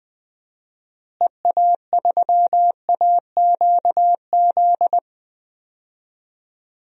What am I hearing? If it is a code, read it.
Morse code, heard as a keyed tone, about 21 wpm, 700 Hz. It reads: EA3AQZ
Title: DXCC-100 Call Signs - 6 Characters - Repeated 2x - 20wpm